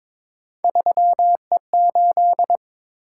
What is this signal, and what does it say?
Morse code, heard as a keyed tone, about 22 wpm, 700 Hz. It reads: 3E8
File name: practice / 22wpm / Three Letter-Number Combinations - 22wpm